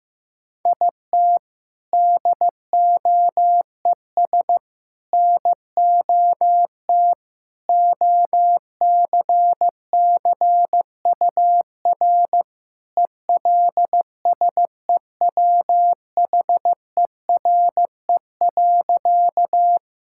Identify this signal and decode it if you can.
Morse code, heard as a keyed tone, about 15 wpm, 700 Hz. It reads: IT DOES NOT OCCUR ELSEWHERE.
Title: Sentences from Top 1600 Words 15wpm